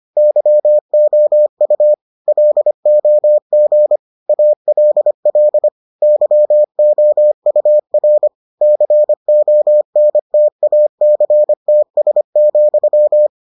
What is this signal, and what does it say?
Morse code, heard as a keyed tone, about 25 wpm, 600 Hz. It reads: YOU LOG ALL YOUR CONTACTS,